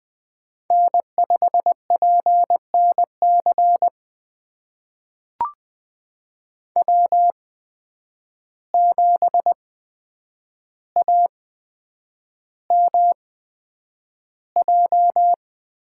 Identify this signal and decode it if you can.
Morse code, heard as a keyed tone, about 20 wpm, spaced as for 9 wpm, 700 Hz. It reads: N5PNC W7AMJ
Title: USA Call Signs - 5 Characters - 8x Character Spacing - 20wpm